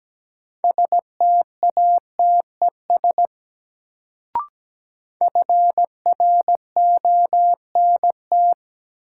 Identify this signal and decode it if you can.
Morse code, heard as a keyed tone, about 17 wpm, 700 Hz. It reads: STATES FRONT
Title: Sets of 2 Words - New Word Drill - 17wpm